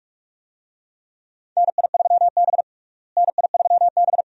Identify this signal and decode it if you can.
Morse code, heard as a keyed tone, about 41 wpm, 700 Hz. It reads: NI3B NI3B
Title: USA Call Signs - 4 Characters - Repeated 2x - 45wpm